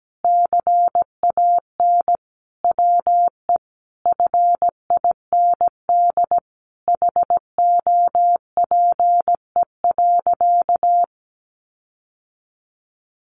Morse code, heard as a keyed tone, about 17 wpm, 700 Hz. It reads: CAN WE FIND HOPE.
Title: Sentences from Top 1300 Words 17wpm